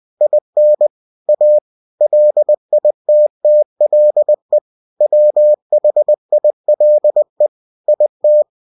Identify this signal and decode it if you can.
Morse code, heard as a keyed tone, about 20 wpm, 600 Hz. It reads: IN A LITTLE WHILE IT
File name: cw-books / winnie-the-pooh / winnie-20wpm-page-0092